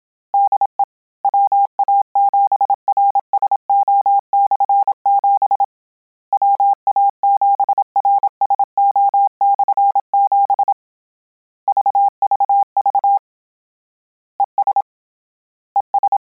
Morse code, heard as a keyed tone, about 27 wpm, 800 Hz. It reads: DE WA7RSO/7 WA7RSO/7 VVV ES ES